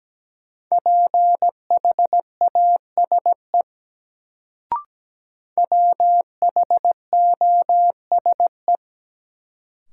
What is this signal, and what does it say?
Morse code, heard as a keyed tone, about 17 wpm, 700 Hz. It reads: PHASE WHOSE
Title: Sentences from Top 1200 Words - New Word Drill - 17wpm